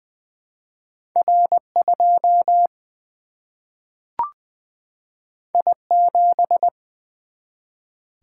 Morse code, heard as a keyed tone, about 20 wpm, 700 Hz. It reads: R2 I7
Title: Single-Letter Single-Number 20wpm